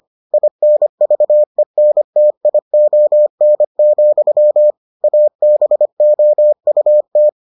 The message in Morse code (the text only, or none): INVENTION, ABOUT